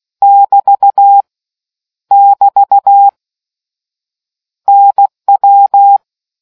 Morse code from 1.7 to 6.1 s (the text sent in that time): <BT> NW